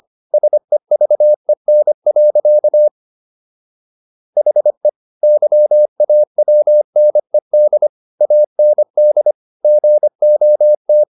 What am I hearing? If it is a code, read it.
Morse code, heard as a keyed tone, about 25 wpm, 600 Hz. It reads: SEVEN. HE YAWNED AND GOT